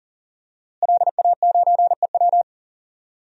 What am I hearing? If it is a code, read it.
Morse code, heard as a keyed tone, about 40 wpm, 700 Hz. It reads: LA9EW
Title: DXCC-100 Call Signs - 5 Characters - Repeated 2x - 40wpm